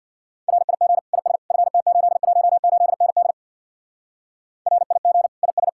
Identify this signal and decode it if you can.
Morse code, heard as a keyed tone, about 61 wpm, 700 Hz. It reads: RIG IS FT818ND RIG IS
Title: Speed-Racing Common QSO Phrases - 50wpm